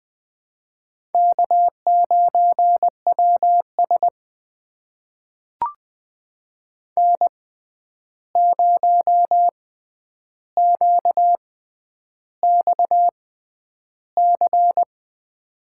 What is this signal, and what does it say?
Morse code, heard as a keyed tone, about 20 wpm, spaced as for 10 wpm, 700 Hz. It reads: K9WS N0QXC
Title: USA Call Signs - 6x Character Spacing - 20wpm